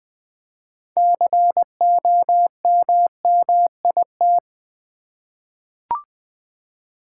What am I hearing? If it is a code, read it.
Morse code, heard as a keyed tone, about 20 wpm, 700 Hz. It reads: COMMIT
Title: Six Letter Words - 20wpm